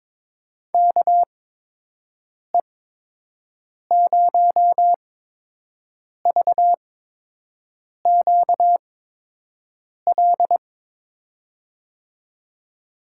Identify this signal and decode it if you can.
Morse code, heard as a keyed tone, about 22 wpm, spaced as for 6 wpm, 700 Hz. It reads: KE0VQL
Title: USA Call Signs - 6 Characters - 8x Character Spacing - 22wpm